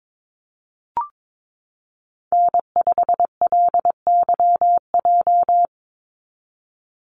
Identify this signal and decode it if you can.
Morse code, heard as a keyed tone, about 22 wpm, 700 Hz. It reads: N5LYJ
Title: Call Signs Repeated 2x - 22wpm